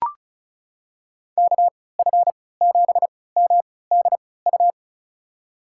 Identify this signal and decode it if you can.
Morse code, heard as a keyed tone, about 34 wpm, spaced as for 20 wpm, 700 Hz. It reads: KF7MDU
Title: USA Call Signs - 6 Characters - 3x Character Spacing - 35wpm